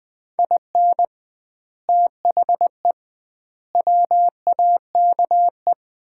Morse code, heard as a keyed tone, about 20 wpm, 700 Hz. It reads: IN THE WAKE
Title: 2x Word-Spacing - Sets of 4 Words - 20wpm